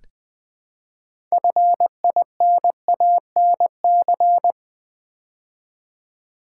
Morse code, heard as a keyed tone, about 20 wpm, 700 Hz. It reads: FINANC
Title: Build A Word - 3 Letters At A Time - 9-Letter Words - 20wpm